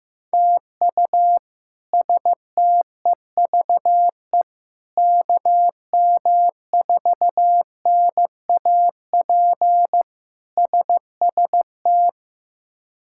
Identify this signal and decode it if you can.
Morse code, heard as a keyed tone, about 15 wpm, 700 Hz. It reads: TU STEVE KM4NAP SST
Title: K1USN SST QSO Practice - 15wpm